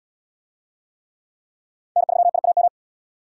Morse code, heard as a keyed tone, about 69 wpm, 700 Hz. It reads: N3IAM